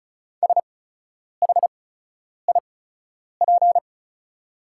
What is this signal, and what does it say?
Morse code, heard as a keyed tone, about 36 wpm, spaced as for 10 wpm, 690 Hz. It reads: SHIP
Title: 8x Character Spacing - Top 500 Words - 35wpm